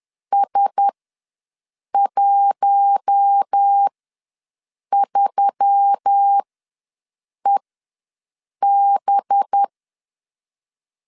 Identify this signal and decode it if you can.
Morse code, heard as a keyed tone, about 10 wpm, spaced as for 6 wpm, 800 Hz. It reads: S13EB
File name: LEZIONI / 5 LBVC / 5 LB ws 10wpm 3616